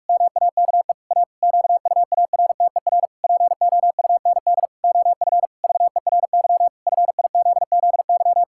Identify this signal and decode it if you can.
Morse code, heard as a keyed tone, about 46 wpm, 700 Hz. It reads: MAKE A QUARTER POUND OF VERY FIZZY
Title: Pangram Challenge v2 - 45wpm